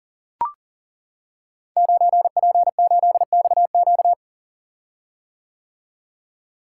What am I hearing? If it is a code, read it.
Morse code, heard as a keyed tone, about 39 wpm, 700 Hz. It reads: 9P8XQ
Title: Code Groups - 5 Letter:Numbers - 40wpm